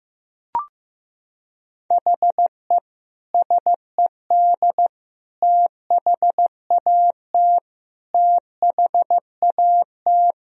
Morse code, heard as a keyed tone, about 16 wpm, 700 Hz. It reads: HE SED THAT THAT
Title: Sentences from Top 200 Words and Abbreviations 15wpm